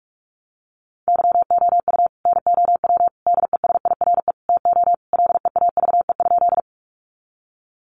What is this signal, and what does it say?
Morse code, heard as a keyed tone, about 44 wpm, 700 Hz. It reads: YOU NOW DESIRE TO LEAVE?
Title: Sentences from Top 1100 Words 45wpm